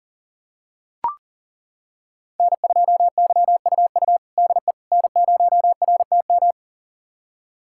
Morse code, heard as a keyed tone, about 37 wpm, 700 Hz. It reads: N2YUU DE N0RTM
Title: USA Callsign DE Callsign - 5 Characters - 40wpm